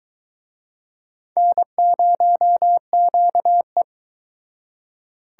Morse code, heard as a keyed tone, about 23 wpm, 700 Hz. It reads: N0QE